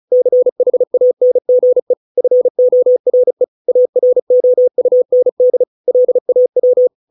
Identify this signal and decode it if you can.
Morse code, heard as a keyed tone, about 34 wpm, 500 Hz. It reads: CHANGE FORE AROUND LAW